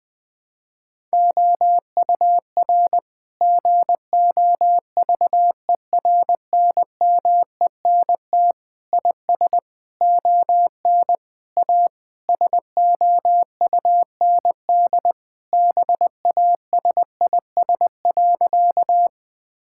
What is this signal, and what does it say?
Morse code, heard as a keyed tone, about 21 wpm, 700 Hz. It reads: OUR GOVERNMENT IS ON A SOUND BASIS.
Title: Sentences from Top 1000 Words - Review Part 5 - 20wpm